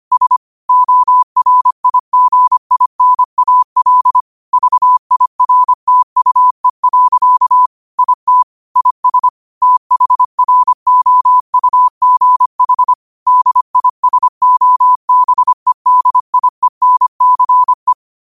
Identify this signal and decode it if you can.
Morse code, heard as a keyed tone, about 25 wpm, 1000 Hz. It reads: S ORIGINAL VIRTUE. IT IS THROUGH DISOBEDIENCE
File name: qotdfiles / 25wpm